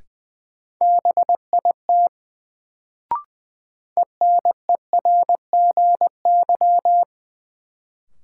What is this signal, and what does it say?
Morse code, heard as a keyed tone, about 20 wpm, 700 Hz. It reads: BIT ENERGY
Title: Sentences from Top 900 Words - New Word Drill - 20wpm